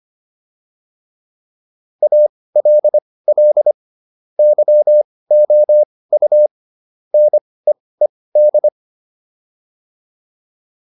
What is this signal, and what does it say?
Morse code, heard as a keyed tone, about 25 wpm, spaced as for 18 wpm, 600 Hz. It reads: ALL YOU NEED